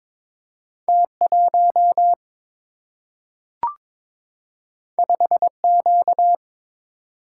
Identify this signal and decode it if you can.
Morse code, heard as a keyed tone, about 22 wpm, 700 Hz. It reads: T1 5Q
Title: Single-Letter Single-Number 22wpm